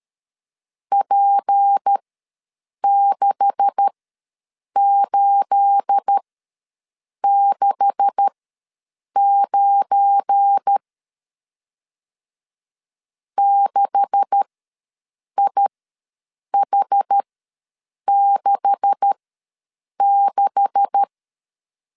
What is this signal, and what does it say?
Morse code, heard as a keyed tone, about 13 wpm, spaced as for 7 wpm, 800 Hz. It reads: P6869 6IH66
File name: LEZIONI / 9 286 ws 12wpm 3351